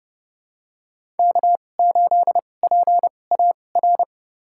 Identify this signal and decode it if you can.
Morse code, heard as a keyed tone, about 32 wpm, spaced as for 22 wpm, 700 Hz. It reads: K8PAR
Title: USA Call Signs - 5 Characters - 2x Character Spacing - 30wpm